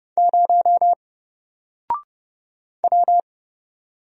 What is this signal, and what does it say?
Morse code, heard as a keyed tone, about 31 wpm, 700 Hz. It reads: 0 W
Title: Single Letter-Number 30wpm